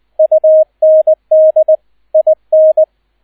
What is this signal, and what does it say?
Morse code, heard as a keyed tone, about 20 wpm, 630 Hz. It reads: UND IN
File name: Erster Streich 20wpm